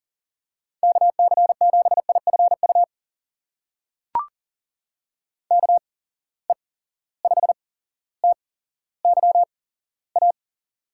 Morse code, heard as a keyed tone, about 40 wpm, spaced as for 17 wpm, 700 Hz. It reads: KC7IFU KE5TYA
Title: USA Call Signs - 6 Characters - 8x Character Spacing - 40wpm